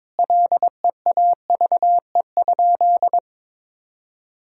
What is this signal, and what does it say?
Morse code, heard as a keyed tone, about 22 wpm, 700 Hz. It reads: LEAVE?
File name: practice / 22wpm / Sentences with 4-Word Phrases - 22wpm